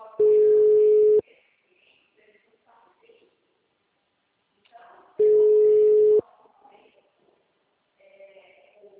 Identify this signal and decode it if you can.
Ringback tone (Europe, 425 Hz), with speech.
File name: out- -2002- - - .142